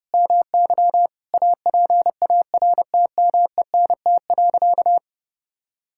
Morse code, heard as a keyed tone, about 32 wpm, 700 Hz. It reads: MY APARTMENT.
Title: Sentences from Top 1000 Words - Review Part 6 - 30wpm